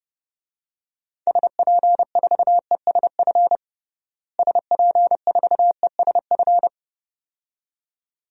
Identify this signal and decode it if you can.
Morse code, heard as a keyed tone, about 32 wpm, 700 Hz. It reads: SP4ESF SP4ESF
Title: DXCC-100 Call Signs - 6 Characters - Repeated 2x - 30wpm